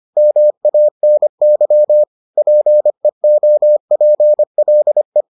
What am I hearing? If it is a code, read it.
Morse code, heard as a keyed tone, about 25 wpm, 600 Hz. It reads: MANY PEOPLE